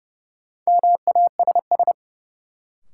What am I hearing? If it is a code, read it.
Morse code, heard as a keyed tone, about 30 wpm, 700 Hz. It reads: MASS
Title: Sets of 5 Words Encore - New Word Drill - 30wpm